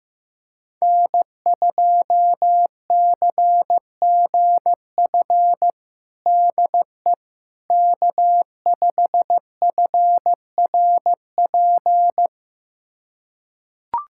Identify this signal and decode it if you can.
Morse code, heard as a keyed tone, about 16 wpm, 700 Hz. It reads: N2CGF DE K5FRP
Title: USA Callsign DE Callsign - 5 Characters - 15wpm